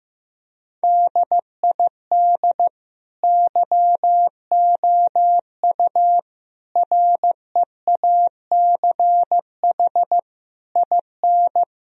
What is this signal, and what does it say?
Morse code, heard as a keyed tone, about 15 wpm, 700 Hz. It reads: DID YOU REACH IN
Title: Sentences from Top 1000 Words - Review Part 3 - 15wpm